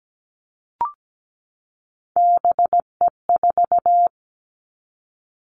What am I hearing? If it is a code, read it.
Morse code, heard as a keyed tone, about 17 wpm, 700 Hz. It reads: BE4
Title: Three Challenging Letter-Number Combinations 17wpm